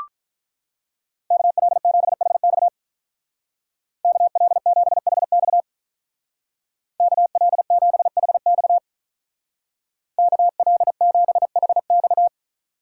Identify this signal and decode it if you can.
Morse code, heard as a keyed tone, about 43 wpm, 700 Hz. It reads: KL7HX KL7HX KL7HX KL7HX
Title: Speed Racing - USA 2x2 Call Signs 35wpm